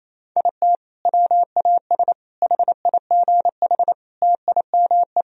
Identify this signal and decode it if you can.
Morse code, heard as a keyed tone, about 28 wpm, 700 Hz. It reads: IT WAS HIGH TIME
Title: Sentences from Top 1000 Words - Review Part 2 - 28wpm